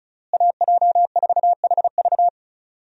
Morse code, heard as a keyed tone, about 36 wpm, 700 Hz. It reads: AJ4HV
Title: USA Call Signs - 5 Characters - Repeated 3x - 35wpm